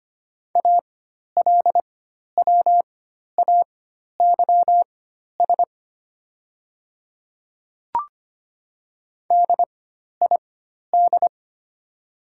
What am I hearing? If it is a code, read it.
Morse code, heard as a keyed tone, about 25 wpm, spaced as for 12 wpm, 700 Hz. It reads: ALWAYS DID
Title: 4x Character Spacing - Top 500 Words - 25wpm